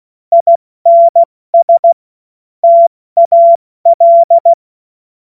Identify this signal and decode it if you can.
Morse code, heard as a keyed tone, about 16 wpm, 680 Hz. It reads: INS TAL
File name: Morsekurs / Witze / Witz 14. Dezember 2021, 70 F80